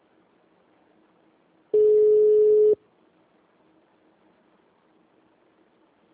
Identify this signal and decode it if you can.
Ringback tone (Europe, 425 Hz).
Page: Marilia - SP